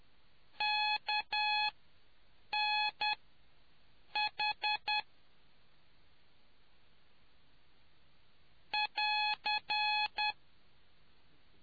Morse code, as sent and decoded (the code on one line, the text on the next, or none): -.- -. .... / .-.-.
KNH <AR>